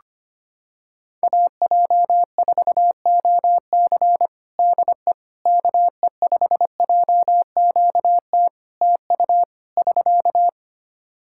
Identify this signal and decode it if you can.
Morse code, heard as a keyed tone, about 25 wpm, 700 Hz. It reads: AJ4OC DE KE5JQT TU <SK>
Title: K1USN SST QSO Practice 25wpm